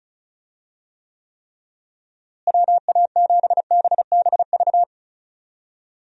Morse code, heard as a keyed tone, about 34 wpm, 700 Hz. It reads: WA7BBV
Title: USA Call Signs - 6 Characters - Repeated 3x - 35wpm